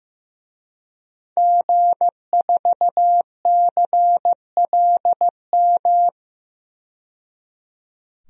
Morse code, heard as a keyed tone, about 15 wpm, 700 Hz. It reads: G4CLM
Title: DXCC-100 Call Signs - 5 Characters - 15wpm